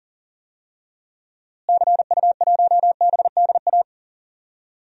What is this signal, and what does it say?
Morse code, heard as a keyed tone, about 40 wpm, 700 Hz. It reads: CU1BDA
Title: DXCC-100 Call Signs - 6 Characters - Repeated 2x - 40wpm